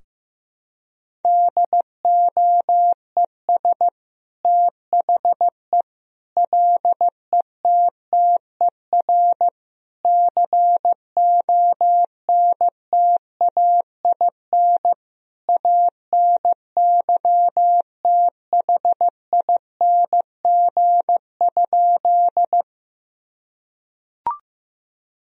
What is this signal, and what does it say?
Morse code, heard as a keyed tone, about 15 wpm, 700 Hz. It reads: DOES THE LETTER CONTAIN ANYTHING?
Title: Sentences from Top 1000 Words - Review Part 2 - 15wpm